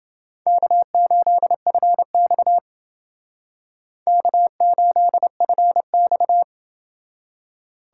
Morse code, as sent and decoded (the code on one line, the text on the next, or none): -.- ---.. ..-. -..- / -.- ---.. ..-. -..-
K8FX K8FX